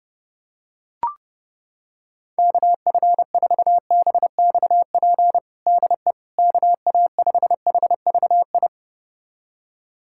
Morse code, heard as a keyed tone, about 31 wpm, 700 Hz. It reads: KF4BXP DE KA5HVI
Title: USA Callsign DE Callsign - 6 Characters - 30wpm